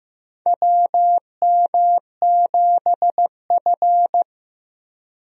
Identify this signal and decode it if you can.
Morse code, heard as a keyed tone, about 16 wpm, 700 Hz. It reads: WM7F
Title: USA Call Signs - 4 Characters - 15wpm